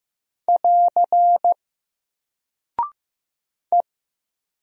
Morse code, heard as a keyed tone, about 15 wpm, 700 Hz. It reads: <AR> E
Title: Single Character (Letter:Number:Punctuation:Prosign) 15wpm